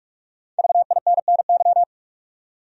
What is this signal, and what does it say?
Morse code, heard as a keyed tone, about 44 wpm, 700 Hz. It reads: VINNY
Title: USA Common Names - Max 5 Letters - 45wpm